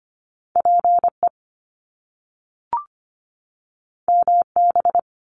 Morse code, heard as a keyed tone, about 25 wpm, 700 Hz. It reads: PE MB
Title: Canadian Provinces and Territories 25wpm